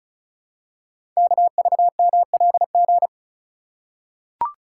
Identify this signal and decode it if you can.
Morse code, heard as a keyed tone, about 36 wpm, 700 Hz. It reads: KVMLG
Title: Code Groups - 5 Letters - 35wpm